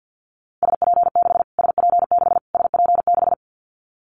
Morse code, heard as a keyed tone, about 49 wpm, 700 Hz. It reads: SP6 SP6 SP6
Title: Callsign Prefixes for Top 25 DXCC - Repeated 3x - 50wpm